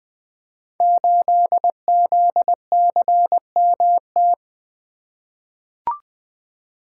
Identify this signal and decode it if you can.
Morse code, heard as a keyed tone, about 20 wpm, 700 Hz. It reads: 8ZCMT
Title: Code Groups - 5 Letter:Numbers - 20wpm